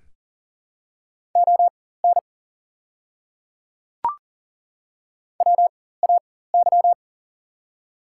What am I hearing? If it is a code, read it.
Morse code, heard as a keyed tone, about 40 wpm, spaced as for 19 wpm, 700 Hz. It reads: ON WAY
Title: 4x Character Spacing - Top 200 Words - 40wpm